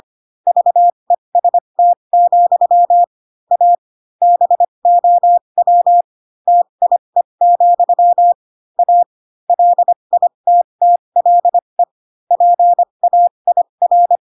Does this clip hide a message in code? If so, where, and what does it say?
Yes: Morse code from 0.5 to 14.2 s, heard as a keyed tone, about 25 wpm, spaced as for 22 wpm, 700 Hz. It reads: VEST, A BOW TIE, A LITTLE PAIR